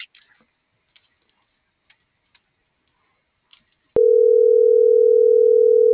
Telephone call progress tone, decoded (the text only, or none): ringback tone